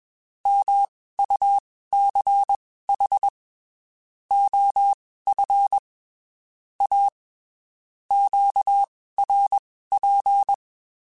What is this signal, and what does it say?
Morse code, heard as a keyed tone, about 21 wpm, spaced as for 15 wpm, 800 Hz. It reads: MUCH OF A QRP
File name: (spaced) / QSOs Spaced-mp3s / WD0BGP 1 @20 sp